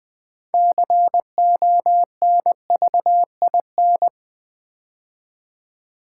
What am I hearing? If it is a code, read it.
Morse code, heard as a keyed tone, about 20 wpm, 700 Hz. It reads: CONVIN
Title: Build A Word - 3 Letters At A Time - 8-Letter Words - 20wpm